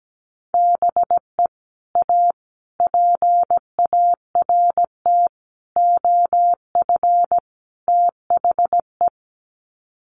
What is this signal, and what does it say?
Morse code, heard as a keyed tone, about 17 wpm, 700 Hz. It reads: BE A PART OF THE
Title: Sets of 5 Words Encore 17wpm